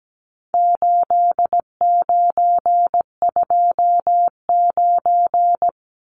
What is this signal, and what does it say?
Morse code, heard as a keyed tone, about 17 wpm, 700 Hz. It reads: 8929
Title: Numbers - 4-Digit 17wpm